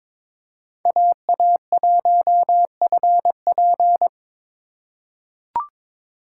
Morse code, heard as a keyed tone, about 22 wpm, 700 Hz. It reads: AA1FP
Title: USA Call Signs - 5 Characters - 2x Character Spacing - 22wpm